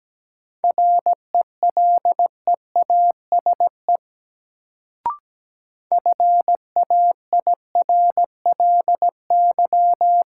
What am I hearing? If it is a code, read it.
Morse code, heard as a keyed tone, about 17 wpm, 700 Hz. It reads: RELEASE FAIRLY
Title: Sentences from Top 1100 Words - New Word Drill - 17wpm